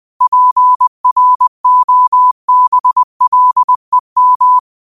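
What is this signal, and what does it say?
Morse code, heard as a keyed tone, about 20 wpm, 1000 Hz. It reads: PROBLEM
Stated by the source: Morse code quote of the day at 20 wpm.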